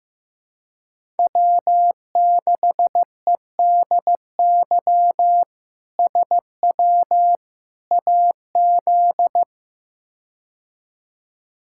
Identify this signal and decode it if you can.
Morse code, heard as a keyed tone, about 15 wpm, 700 Hz. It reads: W6EDY SW AZ
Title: K1USN SST QSO Practice - 15wpm